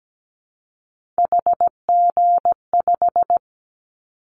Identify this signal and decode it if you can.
Morse code, heard as a keyed tone, about 17 wpm, 700 Hz. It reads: HG5
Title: Callsign Prefixes for Top 50 DXCC - 17wpm